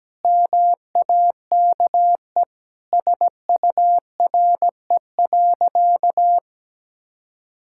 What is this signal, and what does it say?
Morse code, heard as a keyed tone, about 17 wpm, 700 Hz. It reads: MAKE SURE.
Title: Sentences from Top 1000 Words - Review Part 5 - 17wpm